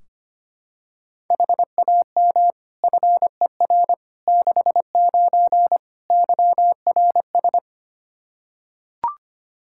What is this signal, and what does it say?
Morse code, heard as a keyed tone, about 25 wpm, 700 Hz. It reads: HAM FER 69 YRS
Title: K1USN SST QSO Practice 25wpm